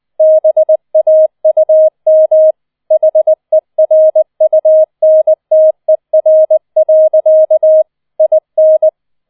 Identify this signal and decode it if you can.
Morse code, heard as a keyed tone, about 20 wpm, 630 Hz. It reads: BAUM HERUNTER. IN
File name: 20wpm